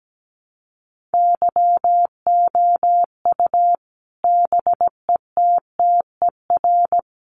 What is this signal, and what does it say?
Morse code, heard as a keyed tone, about 17 wpm, 700 Hz. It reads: YOU BETTER
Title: Sentences from Top 500 Words 17wpm